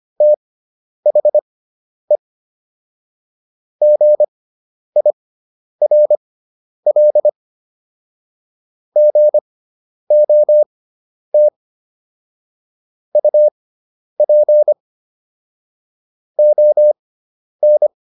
Morse code, heard as a keyed tone, about 25 wpm, spaced as for 10 wpm, 600 Hz. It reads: THE GIRL GOT UP ON